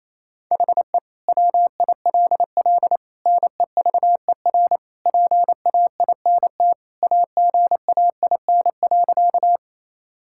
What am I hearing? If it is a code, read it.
Morse code, heard as a keyed tone, about 28 wpm, 700 Hz. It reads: HE WILL NEVER PAINT AGAIN.
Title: Sentences from Top 1000 Words - Review Part 6 - 28wpm